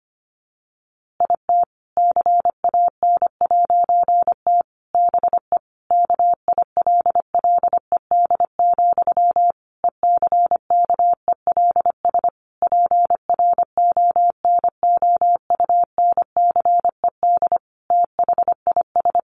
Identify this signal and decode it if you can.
Morse code, heard as a keyed tone, about 25 wpm, 700 Hz. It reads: IT CAN'T BE KILLED, ECKELS PRONOUNCED THIS
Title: A Sound Of Thunder 25wpm